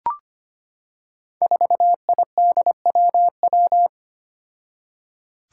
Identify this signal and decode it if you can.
Morse code, heard as a keyed tone, about 25 wpm, 700 Hz. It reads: 4IDWW
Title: Five Challenging Letter-Number Combinations 25wpm